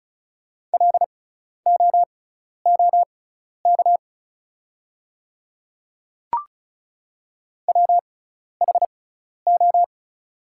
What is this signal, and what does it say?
Morse code, heard as a keyed tone, about 36 wpm, spaced as for 12 wpm, 690 Hz. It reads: LOOK WHO
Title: 6x Character Spacing - Top 100 Words - 35wpm